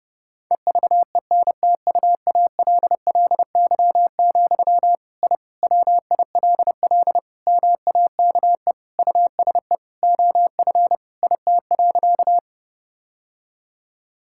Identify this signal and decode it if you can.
Morse code, heard as a keyed tone, about 30 wpm, 700 Hz. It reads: EVENTUALLY, I WILL MAKE USE OF IT.
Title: Sentences from Top 1000 Words - Review Part 6 - 30wpm